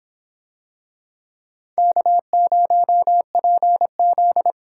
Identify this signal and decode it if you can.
Morse code, heard as a keyed tone, about 26 wpm, 700 Hz. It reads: K0PZ